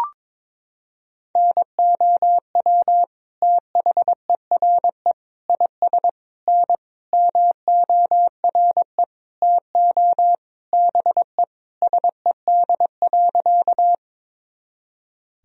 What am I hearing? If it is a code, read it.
Morse code, heard as a keyed tone, about 21 wpm, 700 Hz. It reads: NOW THERE IS N MORE TO BE SED.
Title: Sentences from Top 100 Words and Abbreviations 22wpm